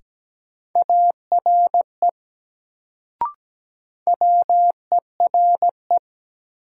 Morse code, heard as a keyed tone, about 17 wpm, 700 Hz. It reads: ARE WERE
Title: Sets of 2 Words - New Word Drill - 17wpm